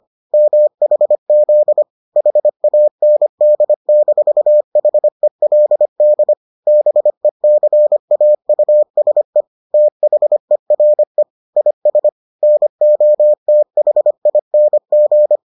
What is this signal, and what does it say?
Morse code, heard as a keyed tone, about 25 wpm, 600 Hz. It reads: MHZ HAND-HELD BECAUSE THERE IS NOTHING